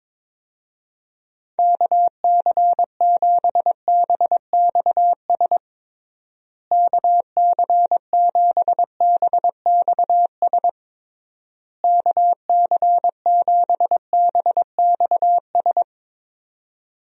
Morse code, heard as a keyed tone, about 22 wpm, 700 Hz. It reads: KC7BXS KC7BXS KC7BXS
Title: USA Call Signs - 6 Characters - Repeated 3x - 22wpm